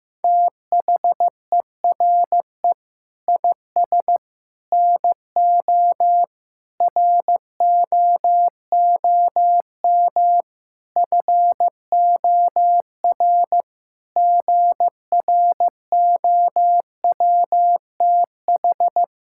Morse code, heard as a keyed tone, about 15 wpm, 700 Hz. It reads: THERE IS NO ROOM FOR GROWTH
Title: Sentences from Top 1000 Words - Review Part 5 - 15wpm